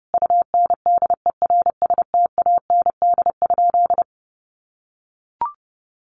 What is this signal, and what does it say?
Morse code, heard as a keyed tone, about 30 wpm, 700 Hz. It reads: UNDERSTAND?
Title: Sentences from Top 1100 Words 30wpm